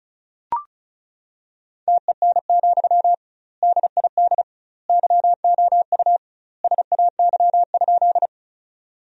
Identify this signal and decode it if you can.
Morse code, heard as a keyed tone, about 36 wpm, 700 Hz. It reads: TEN, DID YOU SAY?
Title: 2x Word-Spacing - Sentences from Top 400 Words - 35wpm